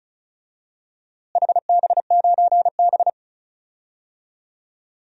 Morse code, heard as a keyed tone, about 36 wpm, 700 Hz. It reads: HB9B
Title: DXCC-100 Call Signs - 4 Characters - Repeated 2x - 35wpm